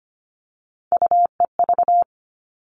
Morse code, heard as a keyed tone, about 25 wpm, 700 Hz. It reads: UEV
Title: All Cut Numbers - 3-Digit 25wpm